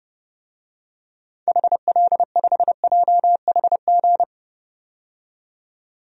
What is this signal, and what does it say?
Morse code, heard as a keyed tone, about 32 wpm, 700 Hz. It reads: HL5JHG
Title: DXCC-100 Call Signs - 6 Characters - 30wpm